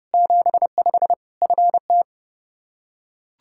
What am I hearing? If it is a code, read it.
Morse code, heard as a keyed tone, about 32 wpm, 700 Hz. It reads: 75 FT
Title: Practice QSO - Exchange 2 - Info - 30wpm